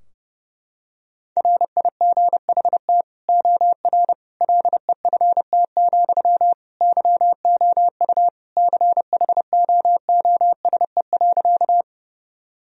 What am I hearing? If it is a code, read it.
Morse code, heard as a keyed tone, about 30 wpm, 700 Hz. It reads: RIGHT OR LEFT, YOU CHOOSE.
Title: Sentences from Top 1000 Words - Review Part 4 - 30wpm